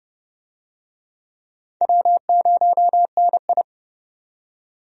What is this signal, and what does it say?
Morse code, heard as a keyed tone, about 31 wpm, 700 Hz. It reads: W0NI
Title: USA Call Signs - 4 Characters - 30wpm